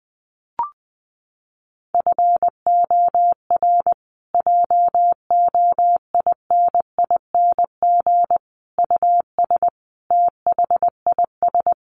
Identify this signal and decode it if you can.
Morse code, heard as a keyed tone, about 20 wpm, 700 Hz. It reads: FOR JOINING US THIS
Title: Sets of 5 Words Encore 20wpm